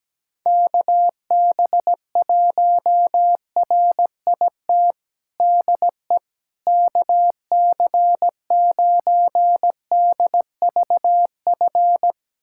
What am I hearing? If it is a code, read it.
Morse code, heard as a keyed tone, about 17 wpm, 700 Hz. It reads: KB1RIT DE KC9DVF
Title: USA Callsign DE Callsign - 6 Characters - 17wpm